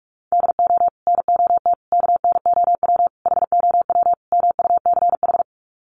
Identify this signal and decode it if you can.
Morse code, heard as a keyed tone, about 44 wpm, 700 Hz. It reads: DO NOT KNOW HOW MUCH
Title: Sets of 5 Words Encore 45wpm